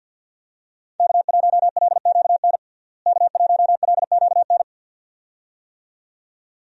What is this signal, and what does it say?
Morse code, heard as a keyed tone, about 52 wpm, 700 Hz. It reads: K1LQN K1LQN
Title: USA Call Signs - 5 Characters - Repeated 2x - 50wpm